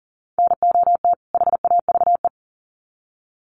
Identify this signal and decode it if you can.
Morse code, heard as a keyed tone, about 40 wpm, 700 Hz. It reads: NOT HAVE
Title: Sets of 2 Words 40wpm